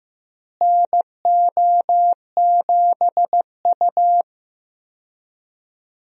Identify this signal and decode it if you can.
Morse code, heard as a keyed tone, about 15 wpm, 700 Hz. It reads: NO7U
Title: USA Call Signs - 4 Characters - Repeated 2x - 15wpm